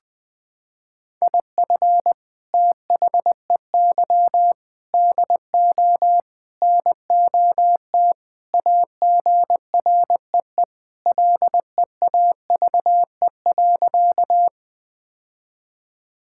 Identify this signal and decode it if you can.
Morse code, heard as a keyed tone, about 20 wpm, 700 Hz. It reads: IF THEY DO NOT AGREE LEAVE.
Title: Sentences with 4-Word Phrases - 20wpm